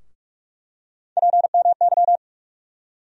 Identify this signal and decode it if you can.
Morse code, heard as a keyed tone, about 44 wpm, 700 Hz. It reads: PMY
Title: USA Call Sign Postfixes - 3 Letters 45wpm